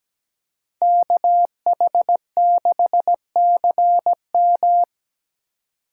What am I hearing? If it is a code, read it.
Morse code, heard as a keyed tone, about 20 wpm, 700 Hz. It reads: KH6CM
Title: Speed Racing - USA 2x2 Call Signs 17wpm